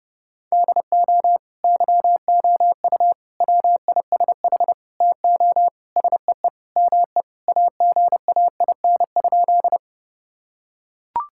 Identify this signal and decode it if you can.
Morse code, heard as a keyed tone, about 30 wpm, 700 Hz. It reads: DO YOU WISH TO SEE ME AGAIN?
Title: Sentences from Top 1600 Words 30wpm